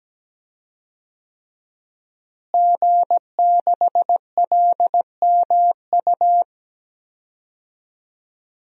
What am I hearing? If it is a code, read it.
Morse code, heard as a keyed tone, about 17 wpm, 700 Hz. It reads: G6LMU
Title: DXCC-100 Call Signs - 5 Characters - Repeated 3x - 17wpm